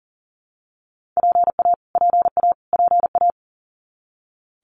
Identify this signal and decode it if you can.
Morse code, heard as a keyed tone, about 40 wpm, 700 Hz. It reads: PA PA PA
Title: US States Repeated 3x 40wpm